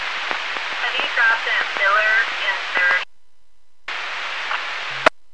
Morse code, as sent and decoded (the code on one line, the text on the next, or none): none